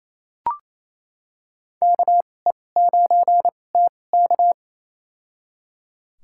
Morse code, heard as a keyed tone, about 28 wpm, spaced as for 20 wpm, 700 Hz. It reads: KE9TK
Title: USA Call Signs - 5 Characters - 2x Character Spacing - 28wpm